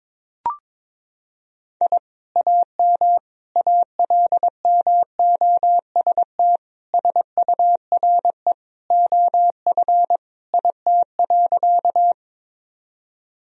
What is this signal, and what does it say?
Morse code, heard as a keyed tone, about 22 wpm, 700 Hz. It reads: I AM ALMOST SURE OF IT.
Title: Sentences from Top 1000 Words - Review Part 4 - 22wpm